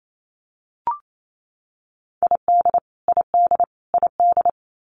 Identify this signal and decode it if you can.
Morse code, heard as a keyed tone, about 28 wpm, 700 Hz. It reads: ID ID ID
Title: US States Repeated 3x 28wpm